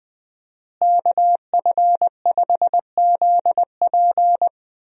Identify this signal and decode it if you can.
Morse code, heard as a keyed tone, about 20 wpm, 700 Hz. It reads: KF5ZP
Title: USA Call Signs - 5 Characters - Repeated 3x - 20wpm